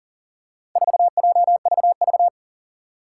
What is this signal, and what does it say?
Morse code, heard as a keyed tone, about 40 wpm, 700 Hz. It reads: 4JVV
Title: Four Challenging Letter-Number Combinations 40wpm